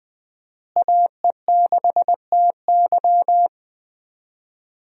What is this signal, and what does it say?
Morse code, heard as a keyed tone, about 20 wpm, 700 Hz. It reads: AE6TY
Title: USA Call Signs - 5 Characters - Repeated 2x - 20wpm